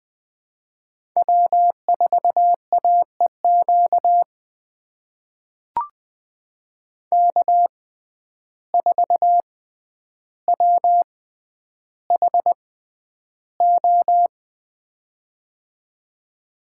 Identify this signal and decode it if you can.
Morse code, heard as a keyed tone, about 20 wpm, spaced as for 10 wpm, 700 Hz. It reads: W4AEQ K4WHO
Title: USA Call Signs - 5 Characters - 6x Character Spacing - 20wpm